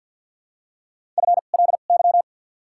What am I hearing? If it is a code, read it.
Morse code, heard as a keyed tone, about 53 wpm, spaced as for 37 wpm, 700 Hz. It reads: FLY
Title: 2x Character Spacing - Top 500 Words - 50wpm